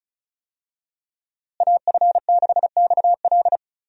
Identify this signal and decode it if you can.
Morse code, heard as a keyed tone, about 36 wpm, 700 Hz. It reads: AF6XL
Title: USA Call Signs - 5 Characters - Repeated 3x - 35wpm